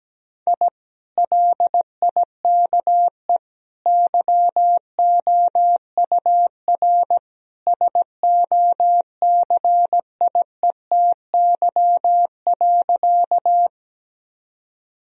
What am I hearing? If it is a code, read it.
Morse code, heard as a keyed tone, about 17 wpm, 700 Hz. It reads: I LIKE YOUR SOCIETY.
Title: Sentences from Top 1000 Words - Review Part 4 - 17wpm